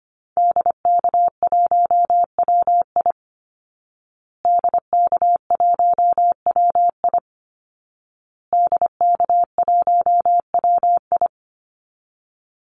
Morse code, heard as a keyed tone, about 25 wpm, 700 Hz. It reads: DK1WI DK1WI DK1WI
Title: Call Signs Repeated 2x - 25wpm